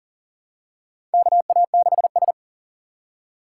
Morse code, heard as a keyed tone, about 38 wpm, 700 Hz. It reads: KA6S
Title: USA Call Signs - 4 Characters - 4x Character Spacing - 40wpm